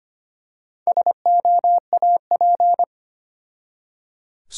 Morse code, heard as a keyed tone, about 25 wpm, 700 Hz. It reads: SOAP
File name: practice / 25wpm / Difficult Words with Hints - 25wpm